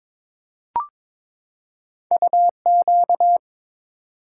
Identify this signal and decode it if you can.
Morse code, heard as a keyed tone, about 22 wpm, 700 Hz. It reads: UQ
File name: practice / 22wpm / Two Letter-Number Combinations - 22wpm